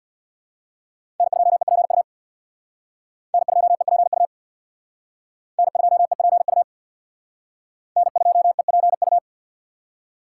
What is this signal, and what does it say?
Morse code, heard as a keyed tone, about 61 wpm, 700 Hz. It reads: N2EPU N2EPU N2EPU N2EPU
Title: Speed Racing - USA 1x3 Call Signs 50wpm